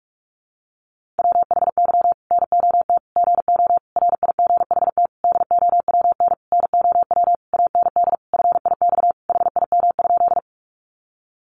Morse code, heard as a keyed tone, about 44 wpm, 700 Hz. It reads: WHY NOT GO RIGHT DOWN NOW AND FIX HIM?
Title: Sentences from Top 1400 Words 45wpm